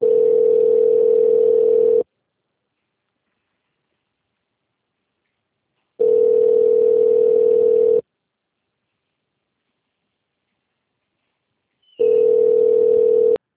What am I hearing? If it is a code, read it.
Ringback tone (North America), 440 + 480 Hz.